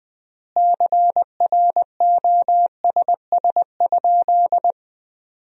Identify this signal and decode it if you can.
Morse code, heard as a keyed tone, about 20 wpm, 700 Hz. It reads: CROSS?
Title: 3x Word-Spacing - Sentences from Top 300 Words - 20wpm